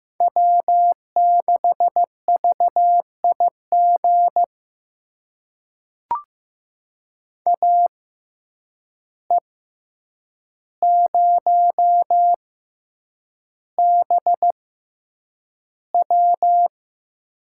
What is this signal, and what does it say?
Morse code, heard as a keyed tone, about 15 wpm, spaced as for 8 wpm, 700 Hz. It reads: W6VIG AE0BW
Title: USA Call Signs - 5 Characters - 6x Character Spacing - 15wpm